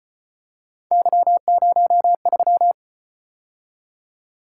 Morse code, heard as a keyed tone, about 34 wpm, 700 Hz. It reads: Y03